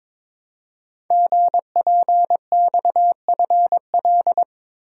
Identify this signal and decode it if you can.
Morse code, heard as a keyed tone, about 23 wpm, 700 Hz. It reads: GPXFL
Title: Code Groups - 5 Letters - 22wpm